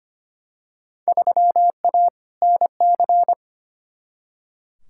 Morse code, heard as a keyed tone, about 25 wpm, 700 Hz. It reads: 3A NC
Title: Field Day Exchanges Encore - 25wpm